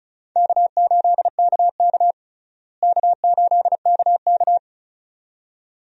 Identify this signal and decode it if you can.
Morse code, heard as a keyed tone, about 34 wpm, 700 Hz. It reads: K8KK K8KK
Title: USA Call Signs - 4 Characters - Repeated 2x - 35wpm